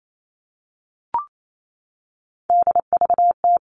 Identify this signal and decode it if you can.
Morse code, heard as a keyed tone, about 28 wpm, 700 Hz. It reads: DVT
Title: All Cut Numbers - 3-Digit 28wpm